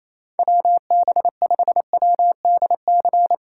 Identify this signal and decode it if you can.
Morse code, heard as a keyed tone, about 28 wpm, 700 Hz. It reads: WB5WDC
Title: USA Call Signs - 6 Characters - 2x Character Spacing - 28wpm